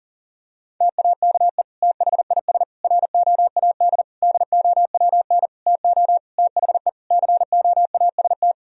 Morse code, heard as a keyed tone, about 40 wpm, 690 Hz. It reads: TAKE THIS ROAD DOWN TO THE COAST
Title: Sentences from Top 1000 Words - Review Part 4 - 40wpm